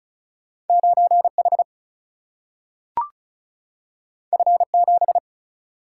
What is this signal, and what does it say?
Morse code, heard as a keyed tone, about 36 wpm, 700 Hz. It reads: 9 H F 7
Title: Single-Letter Single-Number 35wpm